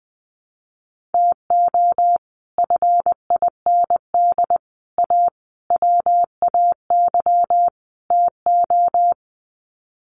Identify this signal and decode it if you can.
Morse code, heard as a keyed tone, about 20 wpm, 700 Hz. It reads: TO FIND A WAY TO
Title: Sets of 5 Words 20wpm